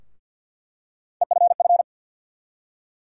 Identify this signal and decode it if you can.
Morse code, heard as a keyed tone, about 46 wpm, 700 Hz. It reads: EFF
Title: USA Call Sign Postfixes - 3 Letters 50wpm